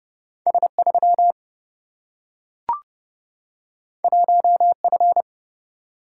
Morse code, heard as a keyed tone, about 30 wpm, 700 Hz. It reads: S3 1F
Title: Single-Letter Single-Number 30wpm